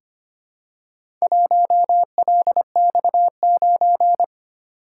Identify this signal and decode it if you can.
Morse code, heard as a keyed tone, about 25 wpm, 700 Hz. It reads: 1LX9
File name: practice / 25wpm / Four Letter-Number Combinations - 25wpm